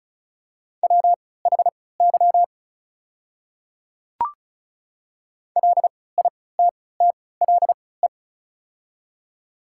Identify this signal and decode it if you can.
Morse code, heard as a keyed tone, about 36 wpm, spaced as for 20 wpm, 700 Hz. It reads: WHY LITTLE
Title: 3x Character Spacing - Top 200 Words - 35wpm